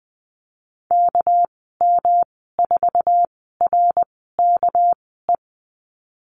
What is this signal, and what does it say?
Morse code, heard as a keyed tone, about 20 wpm, spaced as for 14 wpm, 700 Hz. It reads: KM4RKE
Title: 238 Lesson 38g - Letter q - Call Signs 20wpm